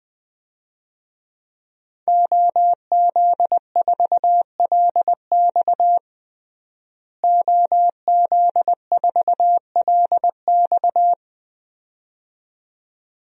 Morse code, heard as a keyed tone, about 21 wpm, 700 Hz. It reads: OZ4LX OZ4LX
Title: DXCC-100 Call Signs - 5 Characters - Repeated 2x - 20wpm